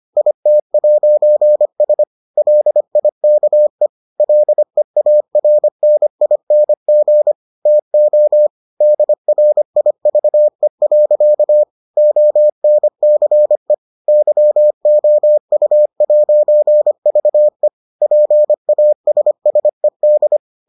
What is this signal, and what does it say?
Morse code, heard as a keyed tone, about 25 wpm, 600 Hz. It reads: IT'S LIKE LEARNING TO DRIVE. ONCE YOU'VE PASSED